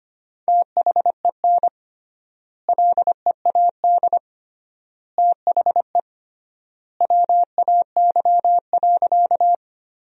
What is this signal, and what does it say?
Morse code, heard as a keyed tone, about 25 wpm, 700 Hz. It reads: THEN LEAD THE WAY.
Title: 3x Word-Spacing - Sentences from Top 500 Words - 25wpm